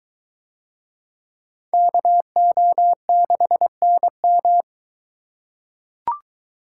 Morse code, heard as a keyed tone, about 23 wpm, 700 Hz. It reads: KO6NM